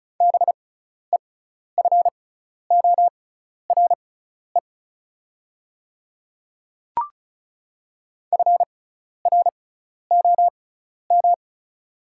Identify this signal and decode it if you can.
Morse code, heard as a keyed tone, about 36 wpm, spaced as for 12 wpm, 700 Hz. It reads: BEFORE FROM
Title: 6x Character Spacing - Top 300 Words - 35wpm